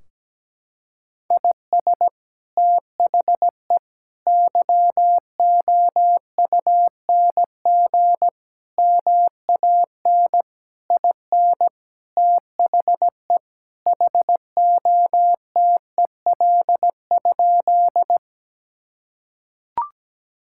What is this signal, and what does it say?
Morse code, heard as a keyed tone, about 17 wpm, 700 Hz. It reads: IS THE YOUNG MAN IN THE HOTEL?
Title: Sentences from Top 1600 Words 17wpm